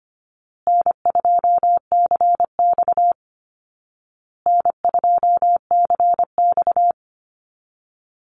Morse code, heard as a keyed tone, about 25 wpm, 700 Hz. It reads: N2CX N2CX
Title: Call Signs Repeated 2x - 25wpm